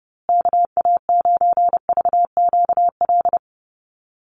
Morse code, heard as a keyed tone, about 30 wpm, 700 Hz. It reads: KA9VQL
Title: USA Call Signs Repeated 3x 30wpm